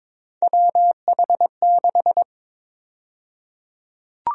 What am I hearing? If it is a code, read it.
Morse code, heard as a keyed tone, about 22 wpm, 700 Hz. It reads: WH6
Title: USA Call Sign Prefixes - Repeated 2x - 22wpm